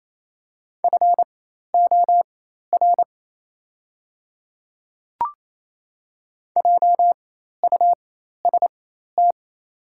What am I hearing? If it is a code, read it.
Morse code, heard as a keyed tone, about 28 wpm, spaced as for 13 wpm, 700 Hz. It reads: FOR JUST
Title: 4x Character Spacing - Top 200 Words - 28wpm